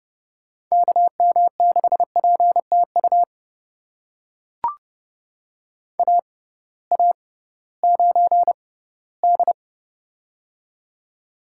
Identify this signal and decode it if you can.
Morse code, heard as a keyed tone, about 30 wpm, spaced as for 18 wpm, 700 Hz. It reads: KM6PTU AA9D
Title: USA Call Signs - 6x Character Spacing - 30wpm